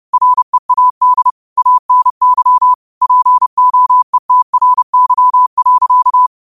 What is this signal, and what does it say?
Morse code, heard as a keyed tone, about 30 wpm, 1000 Hz. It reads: READ ANY POETRY.
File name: qotdfiles / 30wpm